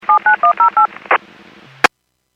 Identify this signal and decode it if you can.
DTMF, heard as telephone touch tones, tones of about 110 ms. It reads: 79108
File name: Businesses / 03495 10-17 Businesses 451.25 WPS